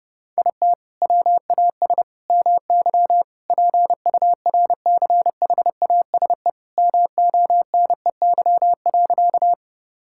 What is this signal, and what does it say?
Morse code, heard as a keyed tone, about 32 wpm, 700 Hz. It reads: IT WAS MY PURCHASE MONEY.
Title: Sentences from Top 1000 Words - Review Part 5 - 30wpm